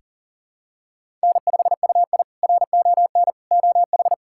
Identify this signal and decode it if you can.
Morse code, heard as a keyed tone, about 37 wpm, 700 Hz. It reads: N5UI RON OH
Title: K1USN SST QSO Practice - 40wpm